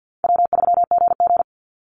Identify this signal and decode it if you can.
Morse code, heard as a keyed tone, about 50 wpm, 700 Hz. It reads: W3GG
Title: Call Signs - 50wpm